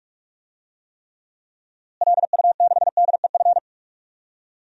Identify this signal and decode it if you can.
Morse code, heard as a keyed tone, about 46 wpm, 700 Hz. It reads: LU6DEF
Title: DXCC-100 Call Signs - 6 Characters - Repeated 2x - 45wpm